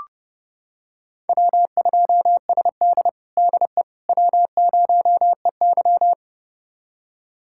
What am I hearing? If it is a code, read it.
Morse code, heard as a keyed tone, about 30 wpm, 700 Hz. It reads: W2SD DE W0EY
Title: USA Callsign DE Callsign - 4 Characters - 30wpm